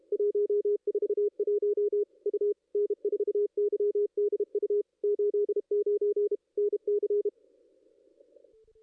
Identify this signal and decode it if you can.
Morse code, heard as a keyed tone, about 31 wpm, 410 Hz. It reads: 141 U N4YDU 89 NC
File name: contests / ss-cw-k5tr-op-n5rz -1700